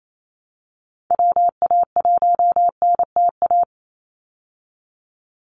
Morse code, heard as a keyed tone, about 28 wpm, 700 Hz. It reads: WA1NTA
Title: Call Signs Repeated 2x - 28wpm